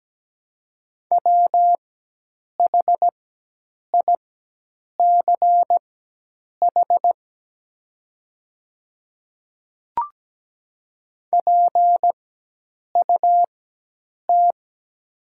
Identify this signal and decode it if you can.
Morse code, heard as a keyed tone, about 17 wpm, spaced as for 8 wpm, 700 Hz. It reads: WHICH PUT
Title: 4x Character Spacing - Top 300 Words - 17wpm